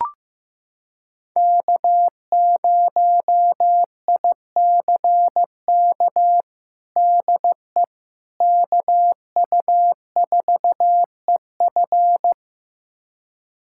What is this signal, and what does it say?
Morse code, heard as a keyed tone, about 15 wpm, 700 Hz. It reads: K0ICK DE KU4EF
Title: USA Callsign DE Callsign - 5 Characters - 15wpm